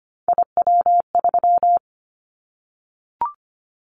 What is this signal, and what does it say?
Morse code, heard as a keyed tone, about 25 wpm, 700 Hz. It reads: IW3
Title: Callsign Prefixes for Top 25 DXCC - Repeated 3x - 25wpm